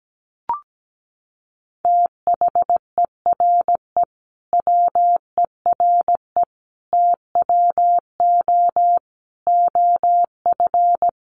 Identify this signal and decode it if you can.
Morse code, heard as a keyed tone, about 17 wpm, 700 Hz. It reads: THERE WERE TWO OF
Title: Sentences from Top 100 Words 17wpm